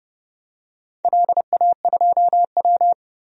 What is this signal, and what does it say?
Morse code, heard as a keyed tone, about 30 wpm, 700 Hz. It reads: LA2W
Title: DXCC-100 Call Signs - 4 Characters - 30wpm